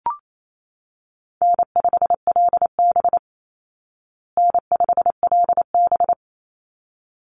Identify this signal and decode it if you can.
Morse code, heard as a keyed tone, about 28 wpm, 700 Hz. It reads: N5LB N5LB
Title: Easy Call Signs Repeated 2x - 28wpm